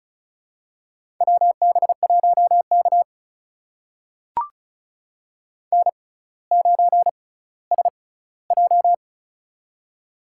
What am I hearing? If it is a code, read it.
Morse code, heard as a keyed tone, about 34 wpm, spaced as for 18 wpm, 700 Hz. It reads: WB1K N9SJ
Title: USA Call Signs - 4 Characters - 6x Character Spacing - 35wpm